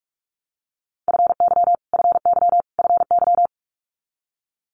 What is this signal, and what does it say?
Morse code, heard as a keyed tone, about 46 wpm, 700 Hz. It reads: FY FY FY
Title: Multi-letter Phonemes Repeated 3x 45wpm